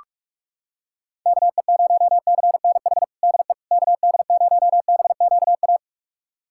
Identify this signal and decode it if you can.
Morse code, heard as a keyed tone, about 41 wpm, 700 Hz. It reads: KE0CNH DE KD0BQA
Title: USA Callsign DE Callsign - 6 Characters - 45wpm